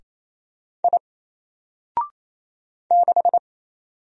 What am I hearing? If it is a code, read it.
Morse code, heard as a keyed tone, about 27 wpm, 700 Hz. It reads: I 6
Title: Single Letter-Number Alt 28wpm